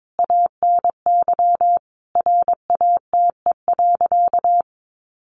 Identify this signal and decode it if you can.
Morse code, heard as a keyed tone, about 22 wpm, 700 Hz. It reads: ANY RATE.
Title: Sentences from Top 900 Words 22wpm